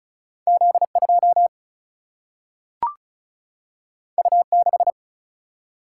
Morse code, heard as a keyed tone, about 36 wpm, 700 Hz. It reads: Z 2 U 6
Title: Single-Letter Single-Number 35wpm